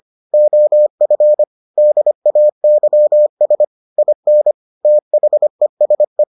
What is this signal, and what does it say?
Morse code, heard as a keyed tone, about 25 wpm, 600 Hz. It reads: OF DAYS IN THESE